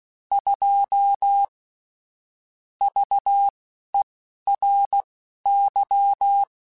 Morse code, heard as a keyed tone, about 16 wpm, spaced as for 11 wpm, 800 Hz. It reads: 2 VERY
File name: Technical Subjects / Technical Subjects-mp3s / Sw Pwr Sup @10